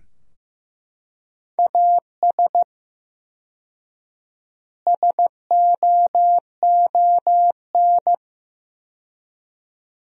Morse code, heard as a keyed tone, about 15 wpm, 700 Hz. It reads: AS SOON
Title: 4x Word-Spacing - Sentences from Top 400 Words - 15wpm